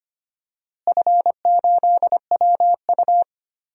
Morse code, heard as a keyed tone, about 25 wpm, 700 Hz. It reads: F8WU
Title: DXCC-100 Call Signs - 4 Characters - Repeated 3x - 25wpm